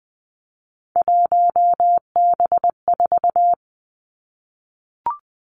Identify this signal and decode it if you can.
Morse code, heard as a keyed tone, about 20 wpm, 700 Hz. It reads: 1B4
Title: Three Challenging Letter-Number Combinations 20wpm